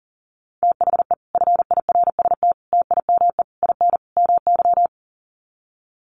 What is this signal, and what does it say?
Morse code, heard as a keyed tone, about 40 wpm, 700 Hz. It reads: THE FIRST TIME IN MY
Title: Sets of 5 Words Encore 40wpm